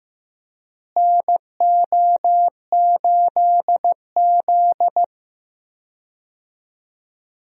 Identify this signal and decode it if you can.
Morse code, heard as a keyed tone, about 16 wpm, 700 Hz. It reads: NO8Z
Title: USA Call Signs - 4 Characters - Repeated 2x - 15wpm